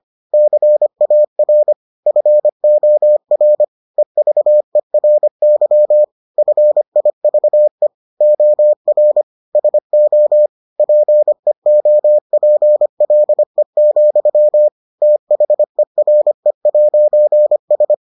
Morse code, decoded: CAR FOR EVERY FIVE OR SO PEOPLE, THERE'S

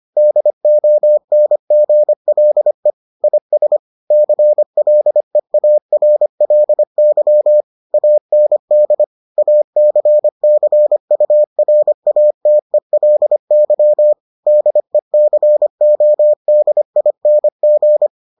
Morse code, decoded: DONGLE IS CLEARLY AND ACCURATELY DECODING

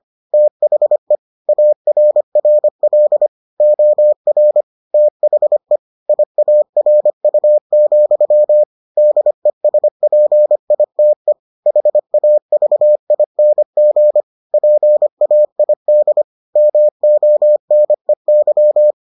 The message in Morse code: THE ARRL OR THE IARU, DESPITE HAVING PAID MONEY

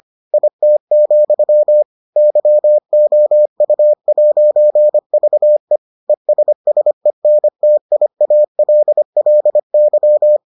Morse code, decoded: IT, YOU'VE ESSENTIALLY

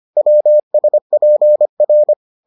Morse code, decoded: WSPR

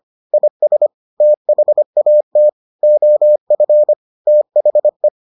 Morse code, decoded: IS THAT OF THE

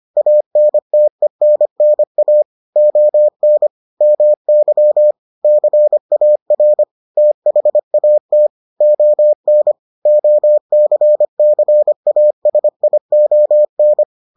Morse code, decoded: ANTENNA ON MY CAR THAT ON OCCASION

